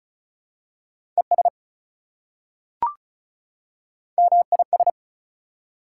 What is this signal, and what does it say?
Morse code, heard as a keyed tone, about 36 wpm, 700 Hz. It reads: ES MIS
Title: Prefixes and Suffixes 35wpm